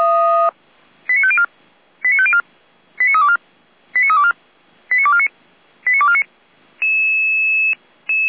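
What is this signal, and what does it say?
Five-tone selective call (ZVEI), heard as quick runs of five tones, tones of about 70 ms. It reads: 89504 89504 89324 89324 89249 89249
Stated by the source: ZVEI five-tone call 89570 received by SDR.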